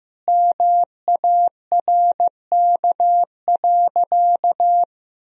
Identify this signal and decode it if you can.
Morse code, heard as a keyed tone, about 15 wpm, 700 Hz. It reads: MARK.
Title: Sentences from Top 1000 Words - Review Part 1 - 15wpm